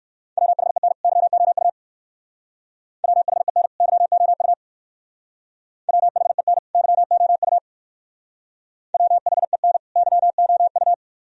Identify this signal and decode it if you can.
Morse code, heard as a keyed tone, about 56 wpm, 700 Hz. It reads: WHEN YOU WHEN YOU WHEN YOU WHEN YOU
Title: Speed-Racing - Sets of 2 Words 45wpm